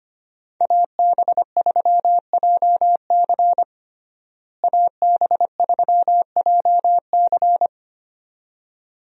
Morse code, heard as a keyed tone, about 25 wpm, 700 Hz. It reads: AB3JC AB3JC
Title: USA Call Signs - 5 Characters - Repeated 2x - 25wpm